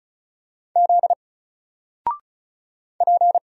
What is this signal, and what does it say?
Morse code, heard as a keyed tone, about 34 wpm, 700 Hz. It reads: Z P
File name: Single-Letter-Number-35wpm